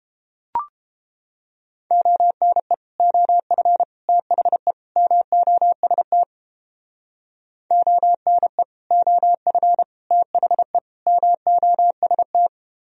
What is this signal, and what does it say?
Morse code, heard as a keyed tone, about 29 wpm, 700 Hz. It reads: ONE OF THE MOST ONE OF THE MOST